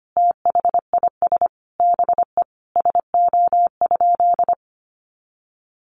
Morse code, heard as a keyed tone, about 25 wpm, 700 Hz. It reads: THIS BE SO?
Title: Sentences from Top 300 Words 25wpm